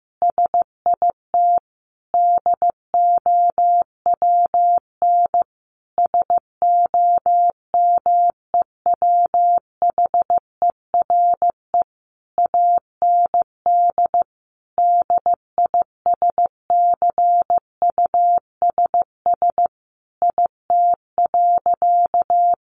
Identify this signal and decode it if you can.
Morse code, heard as a keyed tone, about 15 wpm, 700 Hz. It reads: SIT DOWN SOMEWHERE AND DISCUSS IT.
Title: Sentences from Top 1400 Words 15wpm